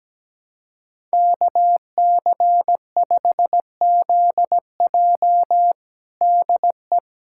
Morse code, heard as a keyed tone, about 17 wpm, 700 Hz. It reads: KC5ZJ DE
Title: USA Callsign DE Callsign - 5 Characters - 17wpm